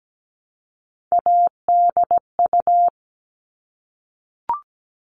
Morse code, heard as a keyed tone, about 17 wpm, 700 Hz. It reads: ADU
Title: All Cut Numbers - 3-Digit 17wpm